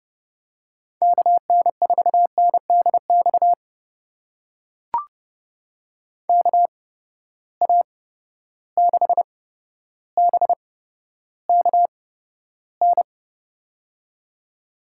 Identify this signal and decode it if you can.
Morse code, heard as a keyed tone, about 32 wpm, spaced as for 13 wpm, 700 Hz. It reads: KN4NDX KA6BKN
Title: USA Call Signs - 8x Character Spacing - 30wpm